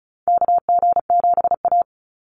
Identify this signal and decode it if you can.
Morse code, heard as a keyed tone, about 36 wpm, 700 Hz. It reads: KG7A
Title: Call Signs - 35wpm